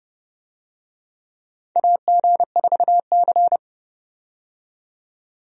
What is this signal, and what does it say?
Morse code, heard as a keyed tone, about 32 wpm, 700 Hz. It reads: AG4C
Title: USA Call Signs - 4 Characters - 30wpm